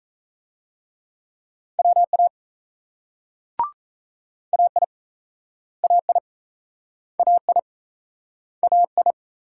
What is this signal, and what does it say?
Morse code, heard as a keyed tone, about 38 wpm, 700 Hz. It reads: WA AI AI AI AI